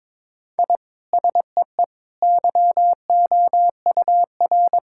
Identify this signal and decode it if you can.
Morse code, heard as a keyed tone, about 22 wpm, 700 Hz. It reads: I SEE YOUR
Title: Sentences from Top 1000 Words - Review Part 5 - 22wpm